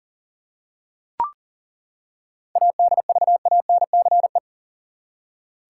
Morse code, heard as a keyed tone, about 40 wpm, 700 Hz. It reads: ADVANCE
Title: Seven Letter Words - 40wpm